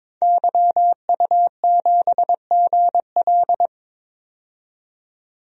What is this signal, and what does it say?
Morse code, heard as a keyed tone, about 22 wpm, 700 Hz. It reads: YU7GL
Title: DXCC-100 Call Signs - 5 Characters - 22wpm